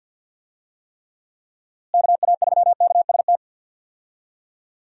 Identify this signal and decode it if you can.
Morse code, heard as a keyed tone, about 46 wpm, 700 Hz. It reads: KA3KST
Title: USA Call Signs - 6 Characters - Repeated 3x - 50wpm